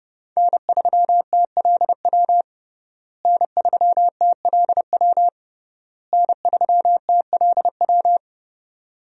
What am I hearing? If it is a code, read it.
Morse code, heard as a keyed tone, about 30 wpm, 700 Hz. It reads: N3TLW N3TLW N3TLW
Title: USA Call Signs - 5 Characters - Repeated 3x - 30wpm